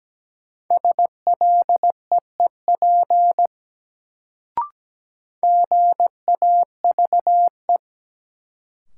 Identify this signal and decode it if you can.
Morse code, heard as a keyed tone, about 17 wpm, 700 Hz. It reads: SLEEP GAVE
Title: Sentences from Top 500 Words - New Word Drill - 17wpm